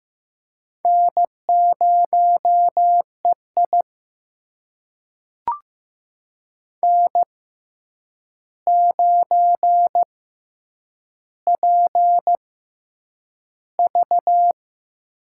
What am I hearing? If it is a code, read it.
Morse code, heard as a keyed tone, about 15 wpm, spaced as for 8 wpm, 700 Hz. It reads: N0EI N9PV
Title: USA Call Signs - 4 Characters - 6x Character Spacing - 15wpm